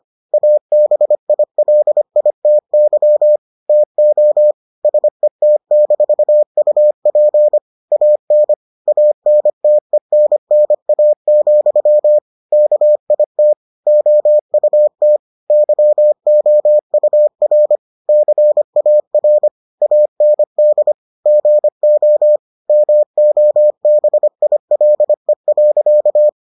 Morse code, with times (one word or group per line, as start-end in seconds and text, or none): 0.3-3.4 ABILITY
3.7-4.5 TO
4.8-7.6 SET-UP
7.9-8.5 AN
8.9-12.2 ANTENNA,
12.5-13.5 KIT
13.9-15.2 OUT
15.5-17.8 YOUR
18.1-19.5 CAR
19.8-20.9 AND
21.3-22.4 GO
22.7-26.3 MOBILE.